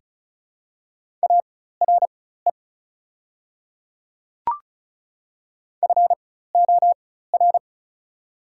Morse code, heard as a keyed tone, about 36 wpm, spaced as for 16 wpm, 690 Hz. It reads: ARE FOR
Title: 4x Character Spacing - Top 100 Words - 35wpm